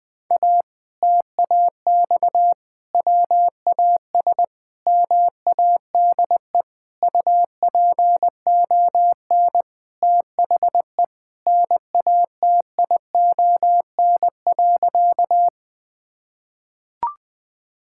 Morse code, heard as a keyed tone, about 21 wpm, 700 Hz. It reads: A TAX WAS MADE UPON THE NATION.
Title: Sentences from Top 1000 Words - Review Part 6 - 20wpm